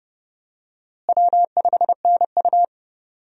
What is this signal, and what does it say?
Morse code, heard as a keyed tone, about 32 wpm, 700 Hz. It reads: W5NU
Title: USA Call Signs - 4 Characters - Repeated 2x - 30wpm